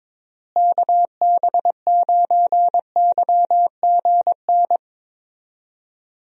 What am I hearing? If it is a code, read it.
Morse code, heard as a keyed tone, about 22 wpm, 700 Hz. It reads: KB9YGN
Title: USA Call Signs - 6 Characters - Repeated 3x - 22wpm